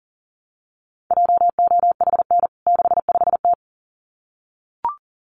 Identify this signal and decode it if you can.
Morse code, heard as a keyed tone, about 40 wpm, 700 Hz. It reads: JOHN 65T
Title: CWT Exchange 40wpm